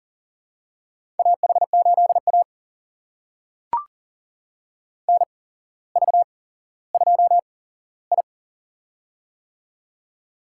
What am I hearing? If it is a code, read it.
Morse code, heard as a keyed tone, about 40 wpm, spaced as for 17 wpm, 700 Hz. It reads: AH8A NV2I
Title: USA Call Signs - 4 Characters - 8x Character Spacing - 40wpm